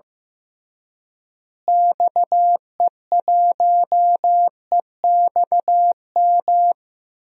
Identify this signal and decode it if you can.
Morse code, heard as a keyed tone, about 15 wpm, 700 Hz. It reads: XE1EXM
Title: DXCC-100 Call Signs - 6 Characters - Repeated 3x - 15wpm